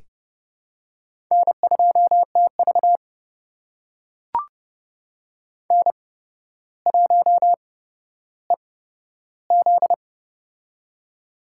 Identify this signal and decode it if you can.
Morse code, heard as a keyed tone, about 30 wpm, spaced as for 13 wpm, 700 Hz. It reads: N2TV N1EZ
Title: USA Call Signs - 8x Character Spacing - 30wpm